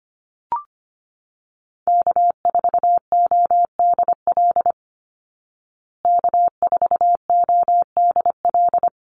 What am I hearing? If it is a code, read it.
Morse code, heard as a keyed tone, about 25 wpm, 700 Hz. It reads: K4ODL K4ODL
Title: Call Signs Repeated 3x - 25wpm